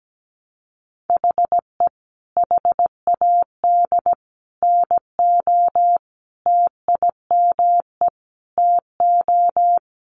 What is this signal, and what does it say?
Morse code, heard as a keyed tone, about 17 wpm, 700 Hz. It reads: HE HAD NO TIME TO
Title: Sentences from Top 1000 Words 17wpm